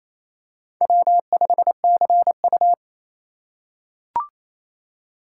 Morse code, heard as a keyed tone, about 28 wpm, 700 Hz. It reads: W5CU
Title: USA Call Signs - 4 Characters - 3x Character Spacing - 28wpm